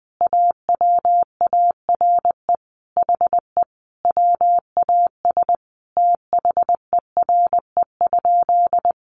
Morse code, heard as a keyed tone, about 20 wpm, 700 Hz. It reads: AWARE HE WAS THERE?
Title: Sentences from Top 1200 Words 20wpm